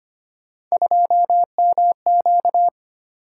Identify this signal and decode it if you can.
Morse code, heard as a keyed tone, about 25 wpm, 700 Hz. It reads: 2MQ
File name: Three Letter-Number Combinations - 25wpm